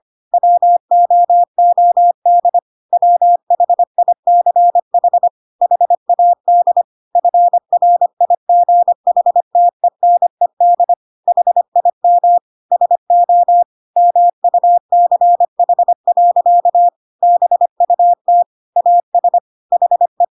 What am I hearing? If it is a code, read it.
Morse code, heard as a keyed tone, about 25 wpm, 700 Hz. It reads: WOOD WHICH HAD FRIGHTENED HIM SO MUCH. BUT AS HE
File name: Pinocchio / Pinocchio ch2 25 25 ik1hge